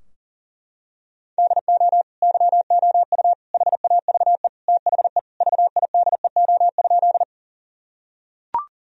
Morse code, heard as a keyed tone, about 40 wpm, 700 Hz. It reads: DO YOU HAVE THE VIDEO?
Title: Sentences from Top 1000 Words - Review Part 4 - 40wpm